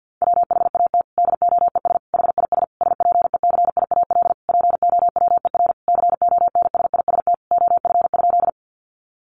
Morse code, heard as a keyed tone, about 50 wpm, 700 Hz. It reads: WHAT DOES HIS SPECIAL POWER CONSIST OF?
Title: Sentences from Top 1400 Words 50wpm